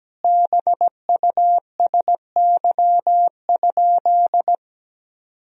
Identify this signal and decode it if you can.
Morse code, heard as a keyed tone, about 17 wpm, 700 Hz. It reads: BUSY?
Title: Sentences from Top 1000 Words - Review Part 3 - 17wpm